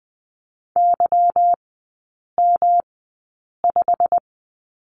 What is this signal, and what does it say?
Morse code, heard as a keyed tone, about 20 wpm, spaced as for 8 wpm, 700 Hz. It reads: YM5
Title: 172 Lesson 28e - Number 9 - Three Characters 20wpm